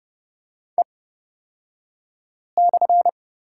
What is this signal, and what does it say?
Morse code, heard as a keyed tone, about 30 wpm, spaced as for 5 wpm, 700 Hz. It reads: E/
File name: Lesson 38d - Letter q - Two Characters 30wpm